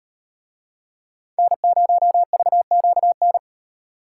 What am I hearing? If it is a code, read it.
Morse code, heard as a keyed tone, about 38 wpm, 700 Hz. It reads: N0VQN